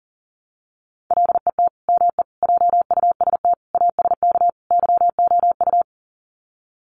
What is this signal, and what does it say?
Morse code, heard as a keyed tone, about 40 wpm, 700 Hz. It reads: LET ME JUST ASK YOU
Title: Sets of 5 Words Encore2 40wpm